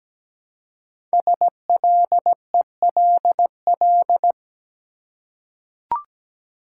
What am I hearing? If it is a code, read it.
Morse code, heard as a keyed tone, about 17 wpm, 700 Hz. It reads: SLELL
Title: Code Groups - 5 Letters - 17wpm